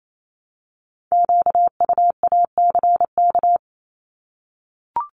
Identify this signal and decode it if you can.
Morse code, heard as a keyed tone, about 28 wpm, 700 Hz. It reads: QUACK
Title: Multi-letter Phonemes Repeated 3x 28wpm